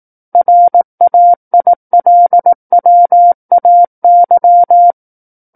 Morse code, heard as a keyed tone, about 18 wpm, 700 Hz. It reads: RAILWAY
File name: 18 WPM